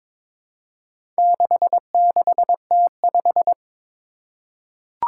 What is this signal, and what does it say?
Morse code, heard as a keyed tone, about 22 wpm, 700 Hz. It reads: 66T5
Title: Four Challenging Letter-Number Combinations 22wpm